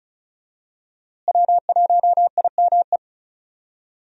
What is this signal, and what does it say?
Morse code, heard as a keyed tone, about 36 wpm, 700 Hz. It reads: W1IME
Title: USA Call Signs - 5 Characters - Repeated 2x - 35wpm